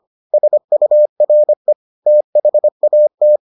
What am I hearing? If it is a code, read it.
Morse code, heard as a keyed tone, about 25 wpm, 600 Hz. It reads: SURE THAT